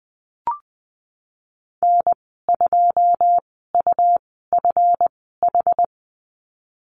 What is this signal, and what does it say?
Morse code, heard as a keyed tone, about 20 wpm, spaced as for 15 wpm, 700 Hz. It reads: N2UFH
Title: 173 Lesson 28g - Number 9 - Call Signs 20wpm